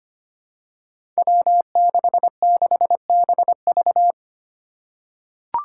Morse code, heard as a keyed tone, about 25 wpm, 700 Hz. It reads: W66BV
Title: Five Challenging Letter-Number Combinations 25wpm